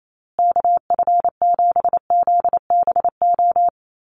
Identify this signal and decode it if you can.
Morse code, heard as a keyed tone, about 28 wpm, 700 Hz. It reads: KF7ZBO
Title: USA Call Signs Repeated 3x 28wpm